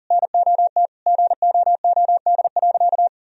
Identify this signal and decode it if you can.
Morse code, heard as a keyed tone, about 40 wpm, 700 Hz. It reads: NOT GOOD.
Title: Sentences from Top 1000 Words - Review Part 1 - 40wpm